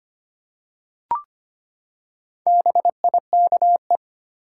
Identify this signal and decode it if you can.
Morse code, heard as a keyed tone, about 25 wpm, 700 Hz. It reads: BIKE
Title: Four Letter Words - 25wpm